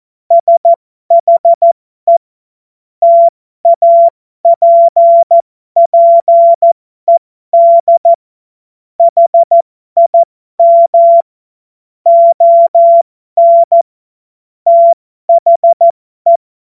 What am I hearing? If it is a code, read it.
Morse code, heard as a keyed tone, about 14 wpm, spaced as for 12 wpm, 680 Hz. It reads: SHE TAPPED HIM ON THE